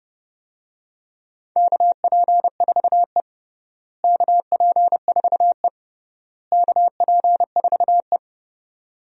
Morse code, heard as a keyed tone, about 30 wpm, 700 Hz. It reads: KP4E KP4E KP4E
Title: DXCC-100 Call Signs - 4 Characters - Repeated 3x - 30wpm